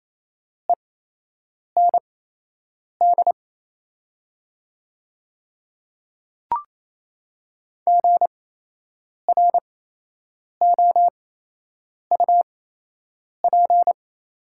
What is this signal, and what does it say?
Morse code, heard as a keyed tone, about 28 wpm, spaced as for 8 wpm, 700 Hz. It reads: END GROUP
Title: 8x Character Spacing - Top 500 Words - 28wpm